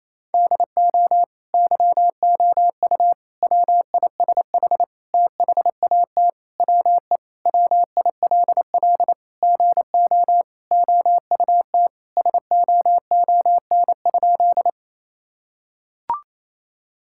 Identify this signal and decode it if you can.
Morse code, heard as a keyed tone, about 28 wpm, 700 Hz. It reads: DO YOU WISH THAT WE WILL GO OUT SOON?
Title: Sentences from Top 1600 Words 28wpm